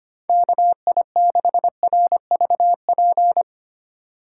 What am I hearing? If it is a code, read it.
Morse code, heard as a keyed tone, about 25 wpm, 700 Hz. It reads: KI6RVP
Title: USA Call Signs - 6 Characters - 8x Character Spacing - 25wpm